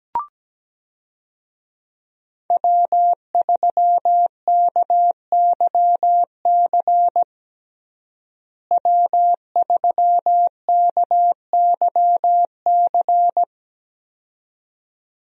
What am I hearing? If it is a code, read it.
Morse code, heard as a keyed tone, about 17 wpm, 700 Hz. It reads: W3KYC W3KYC
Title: USA Call Signs - 5 Characters - Repeated 2x - 17wpm